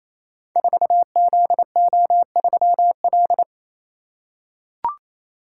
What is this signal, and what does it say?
Morse code, heard as a keyed tone, about 27 wpm, 700 Hz. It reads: 4ZO3L
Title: Code Groups - 5 Letter:Numbers - 28wpm